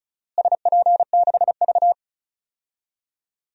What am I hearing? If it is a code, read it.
Morse code, heard as a keyed tone, about 36 wpm, 690 Hz. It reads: SP6V
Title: DXCC-100 Call Signs - 4 Characters - Repeated 3x - 35wpm